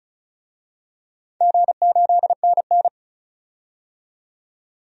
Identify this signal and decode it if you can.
Morse code, heard as a keyed tone, about 36 wpm, 700 Hz. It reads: G8NN
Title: DXCC-100 Call Signs - 4 Characters - 35wpm